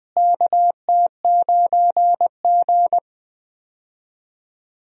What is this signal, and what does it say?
Morse code, heard as a keyed tone, about 20 wpm, 700 Hz. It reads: KT9G
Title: USA Call Signs - 4 Characters - Repeated 2x - 20wpm